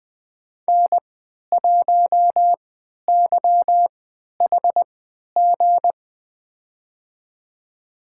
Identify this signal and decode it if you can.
Morse code, heard as a keyed tone, about 20 wpm, spaced as for 11 wpm, 700 Hz. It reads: N1YHG
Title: USA Call Signs - 3x Character Spacing - 20wpm